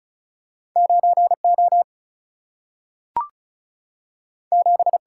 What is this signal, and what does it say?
Morse code, heard as a keyed tone, about 36 wpm, 700 Hz. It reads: 9 O 7
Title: Single-Letter Single-Number 35wpm